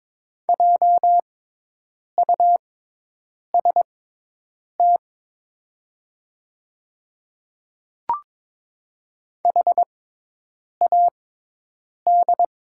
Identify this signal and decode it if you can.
Morse code, heard as a keyed tone, about 22 wpm, spaced as for 8 wpm, 690 Hz. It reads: JUST HAD
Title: 6x Character Spacing - Top 500 Words - 22wpm